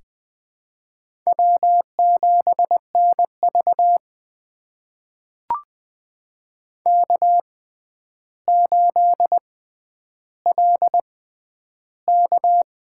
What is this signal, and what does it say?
Morse code, heard as a keyed tone, about 20 wpm, spaced as for 10 wpm, 700 Hz. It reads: W7NV K8LK
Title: USA Call Signs - 6x Character Spacing - 20wpm